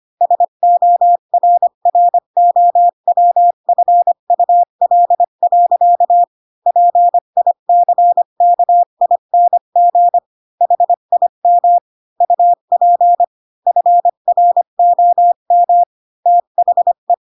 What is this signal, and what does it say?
Morse code, heard as a keyed tone, about 25 wpm, 700 Hz. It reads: SORROWFUL. PICKING HIM UP FROM THE
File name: Pinocchio / Pinocchio ch7 20 25 ik1hge